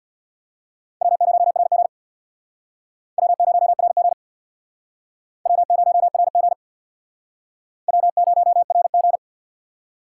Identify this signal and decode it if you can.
Morse code, heard as a keyed tone, about 67 wpm, 700 Hz. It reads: W0RG W0RG W0RG W0RG